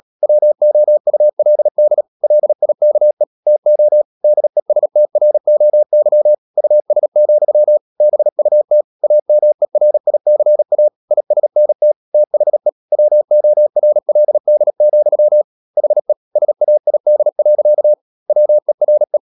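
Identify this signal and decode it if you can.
Morse code, heard as a keyed tone, about 37 wpm, 600 Hz. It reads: WOULD LIKE TO DESTROY US, BUT AMERICA ISNT THE WORLD, HE SAID. WERE
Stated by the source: Morse code practice at 35–39 wpm sending English text of Ken Follett's novel Never, from CW med Gnister.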